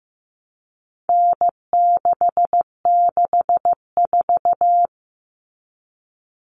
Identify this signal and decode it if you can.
Morse code, heard as a keyed tone, about 15 wpm, 700 Hz. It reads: N664
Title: All Cut Numbers - 4-Digit 15wpm